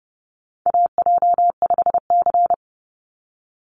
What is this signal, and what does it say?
Morse code, heard as a keyed tone, about 30 wpm, 700 Hz. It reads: AJ5C
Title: NA SOTA Chaser Callsigns 30wpm